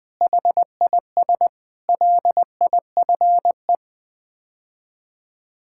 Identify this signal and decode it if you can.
Morse code, heard as a keyed tone, about 20 wpm, 700 Hz. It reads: HIS LIFE
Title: Build A Phrase - 5-Word Phrases - 20wpm